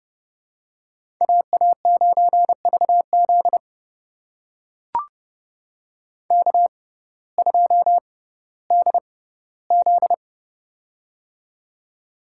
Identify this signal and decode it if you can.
Morse code, heard as a keyed tone, about 30 wpm, spaced as for 16 wpm, 700 Hz. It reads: AA9VZ K2DZ
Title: USA Call Signs - 6x Character Spacing - 30wpm